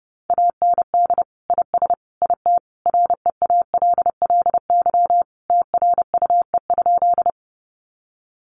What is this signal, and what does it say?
Morse code, heard as a keyed tone, about 30 wpm, 700 Hz. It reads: AND IS IT REALLY TRUE?
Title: Sentences from Top 600 Words 30wpm